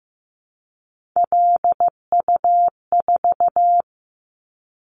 Morse code, heard as a keyed tone, about 15 wpm, 700 Hz. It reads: LU4
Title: Callsign Prefixes for Top 50 DXCC - 15wpm